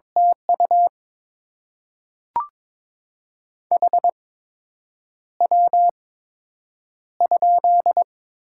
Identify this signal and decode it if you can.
Morse code, heard as a keyed tone, about 23 wpm, spaced as for 8 wpm, 700 Hz. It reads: TU HW?
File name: Top 20 QSO Elements - 8x Character Spacing - 22wpm